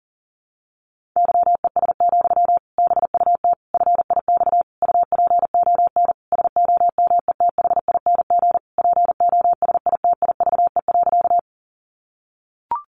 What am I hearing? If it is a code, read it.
Morse code, heard as a keyed tone, about 40 wpm, 700 Hz. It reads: YES, BUT FIX UPON SOMETHING POSITIVE.
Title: Sentences from Top 1400 Words 40wpm